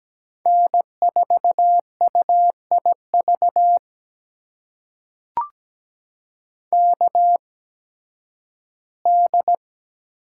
Morse code, heard as a keyed tone, about 17 wpm, spaced as for 11 wpm, 700 Hz. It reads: N4UIV KD
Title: USA Call Signs - 5 Characters - 8x Character Spacing - 17wpm